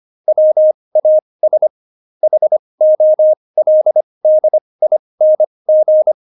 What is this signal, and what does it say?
Morse code, heard as a keyed tone, about 25 wpm, spaced as for 20 wpm, 620 Hz. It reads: WAS HOLDING
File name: Jokes-25-char-20-eff